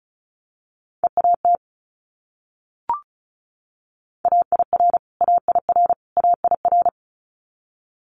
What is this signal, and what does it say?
Morse code, heard as a keyed tone, about 36 wpm, 700 Hz. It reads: EAT AIR AIR AIR
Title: Multi-letter Phonemes Repeated 3x 35wpm